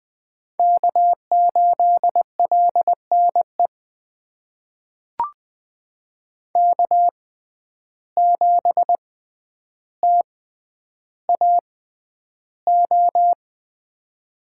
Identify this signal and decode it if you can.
Morse code, heard as a keyed tone, about 20 wpm, spaced as for 10 wpm, 700 Hz. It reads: K8LNE K7TAO
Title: USA Call Signs - 5 Characters - 6x Character Spacing - 20wpm